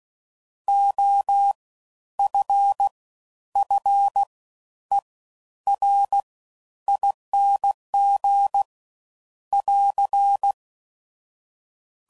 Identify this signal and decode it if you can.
Morse code, heard as a keyed tone, about 16 wpm, spaced as for 10 wpm, 800 Hz. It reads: OFFERING <AR>
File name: Suffixes / Special INGs / Fast Ings